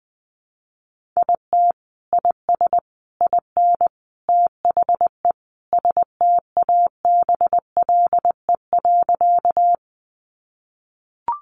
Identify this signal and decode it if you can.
Morse code, heard as a keyed tone, about 20 wpm, 700 Hz. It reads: IT IS IN THE STABLE.
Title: Sentences from Top 1500 Words 20wpm